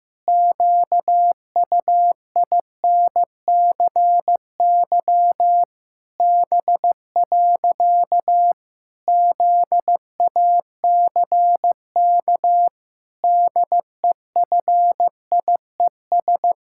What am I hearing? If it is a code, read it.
Morse code, heard as a keyed tone, about 15 wpm, 700 Hz. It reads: QUINCY B. ZACK DEFIES
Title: Pangram Challenge v2 - 15wpm